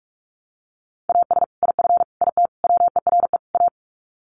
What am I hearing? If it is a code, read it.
Morse code, heard as a keyed tone, about 44 wpm, 700 Hz. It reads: AS IF IT WERE A
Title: Sets of 5 Words 45wpm